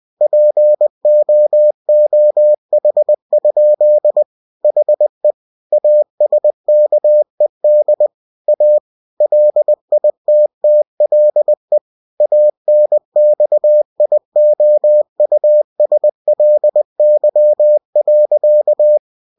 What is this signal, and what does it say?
Morse code, heard as a keyed tone, about 20 wpm, 600 Hz. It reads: POOH? HE ASKED A LITTLE ANXIOUSLY.
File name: cw-books / winnie-the-pooh / winnie-20wpm-page-0052